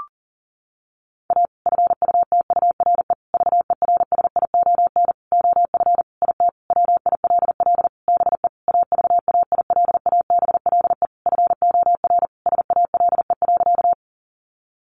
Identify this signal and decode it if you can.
Morse code, heard as a keyed tone, about 40 wpm, 700 Hz. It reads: A FUTURE VERSION OF IT WILL BE AVAILABLE FOR SALE.
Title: Sentences from Top 1400 Words 40wpm